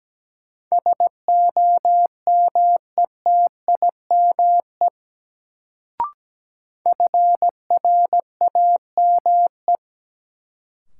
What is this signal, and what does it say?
Morse code, heard as a keyed tone, about 17 wpm, 700 Hz. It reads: SOMETIME FRAME
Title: Sentences from Top 1000 Words - Review Part 6 - New Word Drill - 17wpm